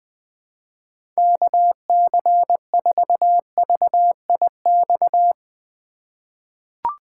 Morse code, heard as a keyed tone, about 20 wpm, 700 Hz. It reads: KC4VIX
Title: USA Call Signs - 6 Characters - 3x Character Spacing - 20wpm